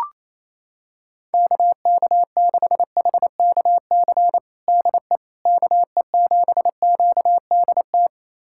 Morse code, heard as a keyed tone, about 28 wpm, 700 Hz. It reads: KK6HKC DE KE7QDT
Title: USA Callsign DE Callsign - 6 Characters - 28wpm